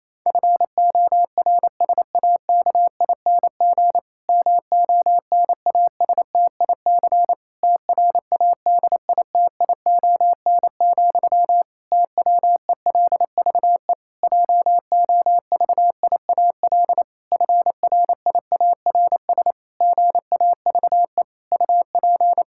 Morse code, heard as a keyed tone, about 27 wpm, 700 Hz. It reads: FORSAKING MONASTIC TRADITION, TWELVE JOVIAL FRIARS GAVE UP
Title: Pangram Challenge v2 - 28wpm